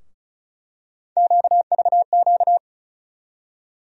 Morse code, heard as a keyed tone, about 36 wpm, 700 Hz. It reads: QVQ
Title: USA Call Sign Postfixes - 3 Letters 35wpm